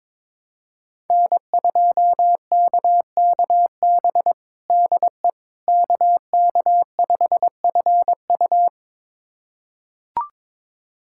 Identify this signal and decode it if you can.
Morse code, heard as a keyed tone, about 22 wpm, 700 Hz. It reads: N2KKB DE KK5FU
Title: USA Callsign DE Callsign 22wpm